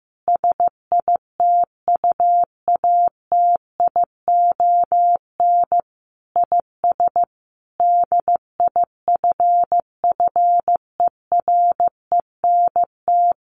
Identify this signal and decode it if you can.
Morse code, heard as a keyed tone, about 15 wpm, 700 Hz. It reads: SITUATION IS DIFFERENT
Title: Sentences from Top 1200 Words 15wpm